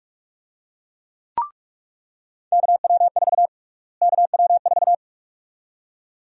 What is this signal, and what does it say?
Morse code, heard as a keyed tone, about 45 wpm, 700 Hz. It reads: KW4 KW4
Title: USA Call Sign Prefixes - Repeated 2x - 45wpm